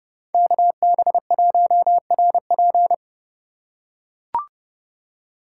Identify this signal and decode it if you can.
Morse code, heard as a keyed tone, about 28 wpm, 700 Hz. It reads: KB1RP